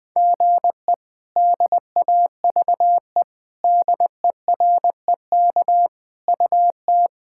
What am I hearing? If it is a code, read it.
Morse code, heard as a keyed tone, about 21 wpm, 700 Hz. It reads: GE DAVE DEREK UT
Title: K1USN SST QSO Practice - 20wpm